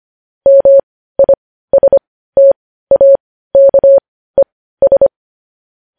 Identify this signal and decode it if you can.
Morse code, heard as a keyed tone, about 25 wpm, spaced as for 15 wpm, 550 Hz. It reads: MISTAKES
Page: news-headlines-in-morse-code-at-15-wpm